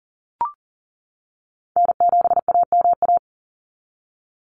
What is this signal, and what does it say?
Morse code, heard as a keyed tone, about 40 wpm, 700 Hz. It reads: N7AMA
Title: NA SOTA Chaser Callsigns 40wpm